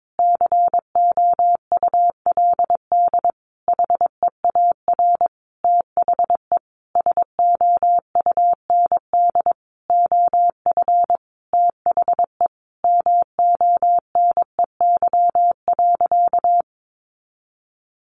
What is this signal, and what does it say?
Morse code, heard as a keyed tone, about 22 wpm, 700 Hz. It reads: COULD HEAR THE SOUND OF THE MONEY.
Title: Sentences from Top 1100 Words 22wpm